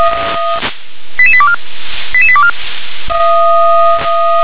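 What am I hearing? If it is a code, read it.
Five-tone selective call (ZVEI), heard as quick runs of five tones, tones of about 70 ms. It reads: 88824 88824
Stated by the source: ZVEI five-tone selective call received on an SDR.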